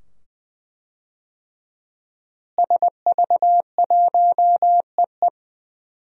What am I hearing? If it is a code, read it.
Morse code, heard as a keyed tone, about 20 wpm, 700 Hz. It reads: SV1EE
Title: DXCC-100 Call Signs - 5 Characters - Repeated 2x - 20wpm